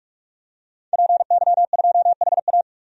Morse code, heard as a keyed tone, about 43 wpm, 700 Hz. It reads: PY2HA
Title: DXCC-100 Call Signs - 5 Characters - 45wpm